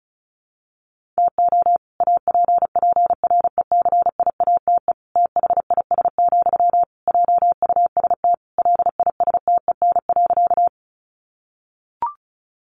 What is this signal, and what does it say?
Morse code, heard as a keyed tone, about 34 wpm, 700 Hz. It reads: TO APPRECIATE THIS, JUST LISTEN.
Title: Sentences from Top 1300 Words 35wpm